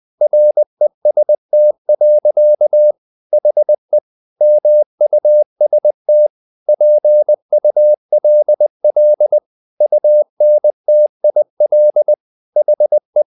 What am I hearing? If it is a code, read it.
Morse code, heard as a keyed tone, about 20 wpm, 600 Hz. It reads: REST. HE MUST PULL UNTIL HE
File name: the old man and the sea part 2 20wpm